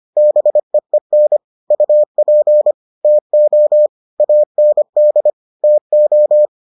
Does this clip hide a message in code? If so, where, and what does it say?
Yes: Morse code from 0.2 to 6.5 s, heard as a keyed tone, about 25 wpm, 600 Hz. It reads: BEEN UP TO AND TO